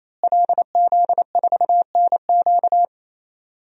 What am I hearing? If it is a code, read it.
Morse code, heard as a keyed tone, about 28 wpm, 700 Hz. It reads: LZ4NQ
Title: DXCC-100 Call Signs - 5 Characters - Repeated 3x - 28wpm